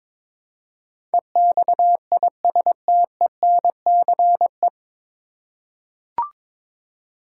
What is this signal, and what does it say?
Morse code, heard as a keyed tone, about 22 wpm, 700 Hz. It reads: EXISTENCE
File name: 9 Letter Words - First 3 Letters Given - 22wpm